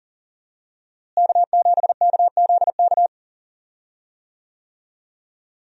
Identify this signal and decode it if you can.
Morse code, heard as a keyed tone, about 40 wpm, 700 Hz. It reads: K7KZK
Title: USA Call Signs - 5 Characters - Repeated 2x - 40wpm